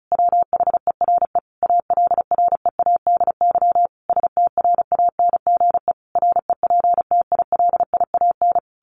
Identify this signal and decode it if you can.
Morse code, heard as a keyed tone, about 34 wpm, 700 Hz. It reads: WHERE ALREADY STRANGE REPTILIAN
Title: A Sound Of Thunder 35wpm